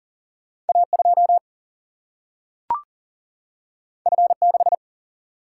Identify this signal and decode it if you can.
Morse code, heard as a keyed tone, about 40 wpm, 700 Hz. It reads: A2 F6
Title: Single-Letter Single-Number 40wpm